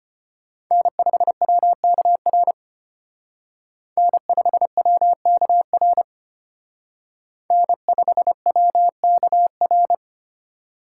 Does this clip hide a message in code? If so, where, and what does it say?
Yes: Morse code from 0.7 to 9.9 s, heard as a keyed tone, about 32 wpm, 700 Hz. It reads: N5WKR N5WKR N5WKR